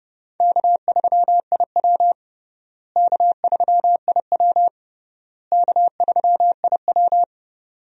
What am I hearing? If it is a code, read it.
Morse code, heard as a keyed tone, about 30 wpm, 700 Hz. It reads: K3IW K3IW K3IW
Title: USA Call Signs - 4 Characters - Repeated 3x - 30wpm